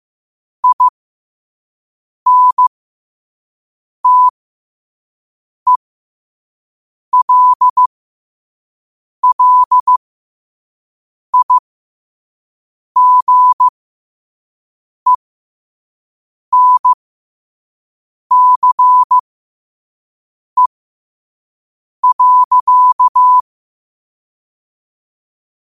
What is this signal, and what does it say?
Morse code, heard as a keyed tone, about 15 wpm, spaced as for 5 wpm, 1000 Hz. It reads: INTELLIGENCE.
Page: Quotes for Fri, 15 Aug 2025 in Morse Code at 5 words per minute.